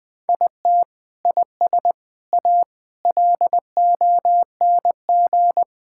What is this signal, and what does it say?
Morse code, heard as a keyed tone, about 20 wpm, 700 Hz. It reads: IT IS A LONG
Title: Sentences from Top 1000 Words - Review Part 5 - 20wpm